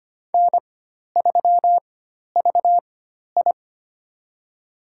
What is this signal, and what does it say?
Morse code, heard as a keyed tone, about 25 wpm, spaced as for 12 wpm, 700 Hz. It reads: N3VI
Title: USA Call Signs - 4 Characters - 4x Character Spacing - 25wpm